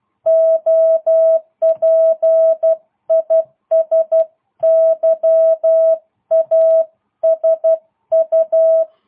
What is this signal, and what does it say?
Morse code, heard as a keyed tone, about 12 wpm, 650 Hz. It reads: OP IS YASU